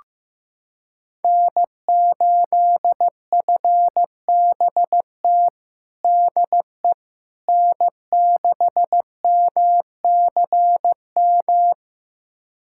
Morse code, heard as a keyed tone, about 15 wpm, 700 Hz. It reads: N8FBT DE N6MCM
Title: USA Callsign DE Callsign - 5 Characters - 15wpm